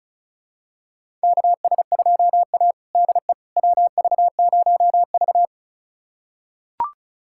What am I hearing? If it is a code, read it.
Morse code, heard as a keyed tone, about 36 wpm, 700 Hz. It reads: KS2A DE WV0V
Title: USA Callsign DE Callsign - 4 Characters - 35wpm